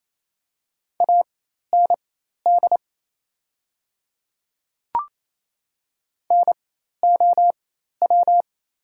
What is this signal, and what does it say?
Morse code, heard as a keyed tone, about 28 wpm, spaced as for 13 wpm, 690 Hz. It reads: AND NOW
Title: 4x Character Spacing - Top 100 Words - 28wpm